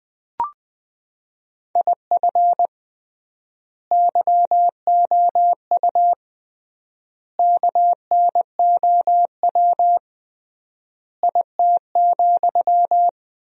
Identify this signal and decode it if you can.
Morse code, heard as a keyed tone, about 20 wpm, 700 Hz. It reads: IF YOU KNOW IT,
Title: 3x Word-Spacing - Sentences from Top 400 Words - 20wpm